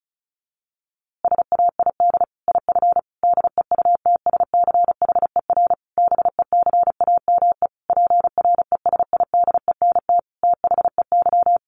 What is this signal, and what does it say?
Morse code, heard as a keyed tone, about 34 wpm, 700 Hz. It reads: SAID IF DEUTSCHER BECAME PRESIDENT THEY
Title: A Sound Of Thunder 35wpm